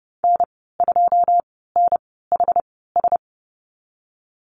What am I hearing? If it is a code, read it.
Morse code, heard as a keyed tone, about 30 wpm, spaced as for 17 wpm, 700 Hz. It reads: N2NHS
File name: Lesson 14g - Number 2 - Call Signs 30wpm